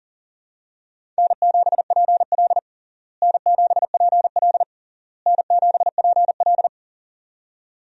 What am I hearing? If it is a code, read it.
Morse code, heard as a keyed tone, about 40 wpm, 700 Hz. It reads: N7PL N7PL N7PL
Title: USA Call Signs - 4 Characters - Repeated 3x - 40wpm